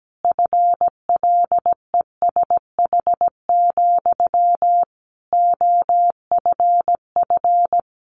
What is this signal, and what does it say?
Morse code, heard as a keyed tone, about 17 wpm, 700 Hz. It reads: FLESH, OFF
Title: A Sound Of Thunder 17wpm